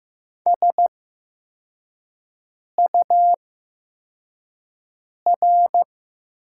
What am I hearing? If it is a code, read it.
Morse code, heard as a keyed tone, about 15 wpm, spaced as for 4 wpm, 700 Hz. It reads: SUR
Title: 8x Character Spacing - Top 500 Words - 15wpm